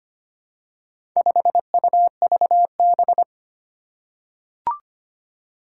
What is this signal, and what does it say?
Morse code, heard as a keyed tone, about 25 wpm, 700 Hz. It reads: 5UVB
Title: Four Challenging Letter-Number Combinations 25wpm